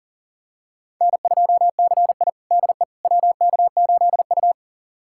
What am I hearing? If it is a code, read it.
Morse code, heard as a keyed tone, about 37 wpm, 700 Hz. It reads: N2CI DE WK8U
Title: USA Callsign DE Callsign - 4 Characters - 40wpm